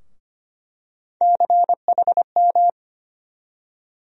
Morse code, heard as a keyed tone, about 25 wpm, 700 Hz. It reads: CHM
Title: USA Call Sign Postfixes - 3 Letters 25wpm